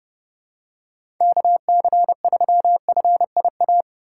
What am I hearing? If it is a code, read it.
Morse code, heard as a keyed tone, about 31 wpm, 700 Hz. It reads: KC3FIA
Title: USA Call Signs - 2x Character Spacing - 30wpm